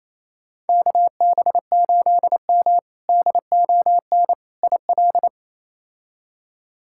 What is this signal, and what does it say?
Morse code, heard as a keyed tone, about 27 wpm, 700 Hz. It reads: KB8M DON IL
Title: K1USN SST QSO Practice - 28wpm